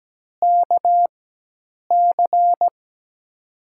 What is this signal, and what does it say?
Morse code, heard as a keyed tone, about 17 wpm, spaced as for 8 wpm, 700 Hz. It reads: KC
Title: USA Call Signs - 6 Characters - 4x Character Spacing - 17wpm